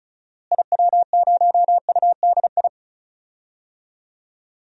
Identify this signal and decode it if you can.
Morse code, heard as a keyed tone, about 34 wpm, 700 Hz. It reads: IW0UDI
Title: DXCC-100 Call Signs - 6 Characters - Repeated 2x - 35wpm